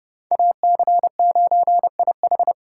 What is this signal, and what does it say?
Morse code, heard as a keyed tone, about 30 wpm, 690 Hz. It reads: AC9IH
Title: USA Call Signs - 5 Characters - 6x Character Spacing - 30wpm